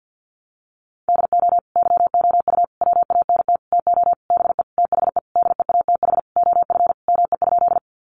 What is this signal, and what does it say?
Morse code, heard as a keyed tone, about 50 wpm, 700 Hz. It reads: DO YOU WANT TO BE THE DEATH OF ME?
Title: Sentences from Top 1100 Words 50wpm